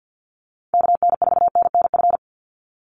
Morse code, heard as a keyed tone, about 50 wpm, 700 Hz. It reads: KN4NNF
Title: USA Call Signs 50wpm